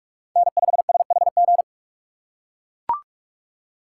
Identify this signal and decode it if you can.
Morse code, heard as a keyed tone, about 44 wpm, 700 Hz. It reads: N5SHG
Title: USA Call Signs - 5 Characters - 6x Character Spacing - 45wpm